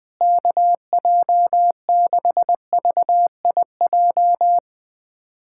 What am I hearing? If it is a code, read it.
Morse code, heard as a keyed tone, about 21 wpm, 700 Hz. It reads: KJ6VIJ
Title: USA Callsign DE Callsign - 6 Characters - 20wpm